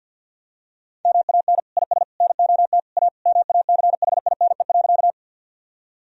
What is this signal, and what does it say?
Morse code, heard as a keyed tone, about 51 wpm, 700 Hz. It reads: MAN IS NOT A MACHINE.
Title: Sentences from Top 1000 Words - Review Part 2 - 50wpm